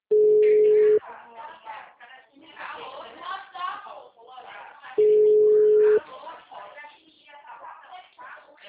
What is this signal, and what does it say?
Ringback tone (Europe, 425 Hz), with speech.